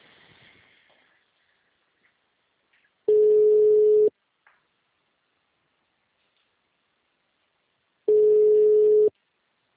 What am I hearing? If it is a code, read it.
Ringback tone (Europe, 425 Hz).
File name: out- -2006- - - .1816